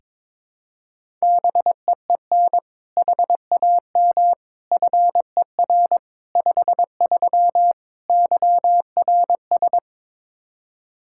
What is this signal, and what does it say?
Morse code, heard as a keyed tone, about 22 wpm, 700 Hz. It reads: BEEN HAM FER 53 YRS
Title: Practice QSO - Exchange 3 - More Info - 22wpm